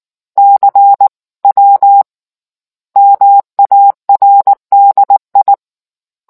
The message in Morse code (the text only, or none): CW MARDI